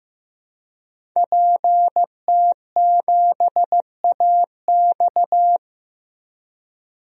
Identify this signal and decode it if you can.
Morse code, heard as a keyed tone, about 15 wpm, 700 Hz. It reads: PT7AX
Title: DXCC-100 Call Signs - 5 Characters - 15wpm